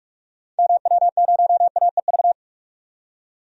Morse code, heard as a keyed tone, about 45 wpm, 700 Hz. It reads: MW0AEV
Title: DXCC-100 Call Signs - 6 Characters - 45wpm